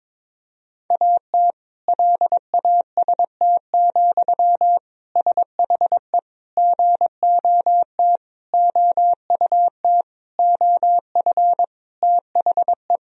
Morse code, heard as a keyed tone, about 22 wpm, 700 Hz. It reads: AT LAST, SHE GOT OUT OF THE
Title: Sentences from Top 1000 Words - Review Part 2 - 22wpm